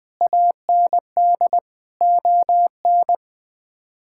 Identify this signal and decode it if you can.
Morse code, heard as a keyed tone, about 20 wpm, 700 Hz. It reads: AND ON
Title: Sets of 8 Words - 20wpm